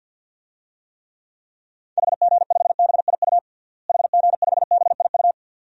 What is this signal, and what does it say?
Morse code, heard as a keyed tone, about 46 wpm, 700 Hz. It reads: HG5BIU HG5BIU
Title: DXCC-100 Call Signs - 6 Characters - Repeated 3x - 50wpm